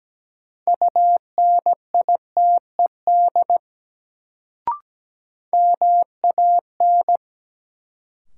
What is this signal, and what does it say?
Morse code, heard as a keyed tone, about 17 wpm, 700 Hz. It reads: UNITED MAN
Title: Sets of 2 Words - New Word Drill - 17wpm